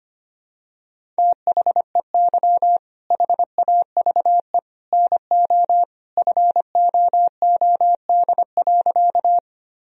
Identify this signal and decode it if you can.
Morse code, heard as a keyed tone, about 25 wpm, 700 Hz. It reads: THEY HAVE NO FOOD.
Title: Sentences from Top 1000 Words - Review Part 1 - 25wpm